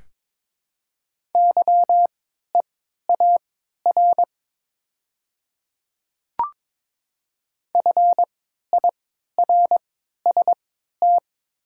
Morse code, heard as a keyed tone, about 22 wpm, spaced as for 12 wpm, 700 Hz. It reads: YEAR FIRST
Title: 3x Character Spacing - Top 300 Words - 22wpm